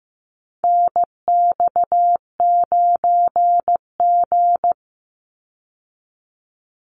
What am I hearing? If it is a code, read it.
Morse code, heard as a keyed tone, about 15 wpm, 700 Hz. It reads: NX9G
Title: Call Signs Repeated 3x - 15wpm